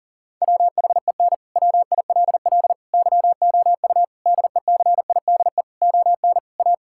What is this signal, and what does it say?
Morse code, heard as a keyed tone, about 40 wpm, 690 Hz. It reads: WHEN WILL YOU DECIDE ON A
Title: Sentences from Top 1000 Words - Review Part 6 - 40wpm